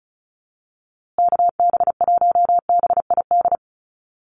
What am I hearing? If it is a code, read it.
Morse code, heard as a keyed tone, about 34 wpm, 700 Hz. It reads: KB1BID
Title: USA Call Signs Repeated 2x 35wpm